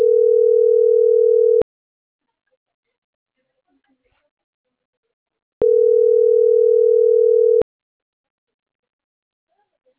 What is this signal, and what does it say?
Ringback tone (North America), 440 + 480 Hz.